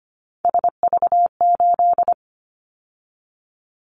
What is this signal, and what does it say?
Morse code, heard as a keyed tone, about 25 wpm, 700 Hz. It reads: SV8
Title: Callsign Prefixes for Top 50 DXCC - 25wpm